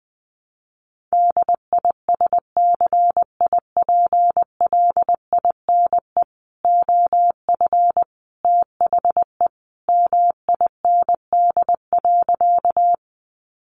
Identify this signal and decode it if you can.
Morse code, heard as a keyed tone, about 20 wpm, 700 Hz. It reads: DISCIPLINE OF THE MIND.
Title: Sentences from Top 900 Words 20wpm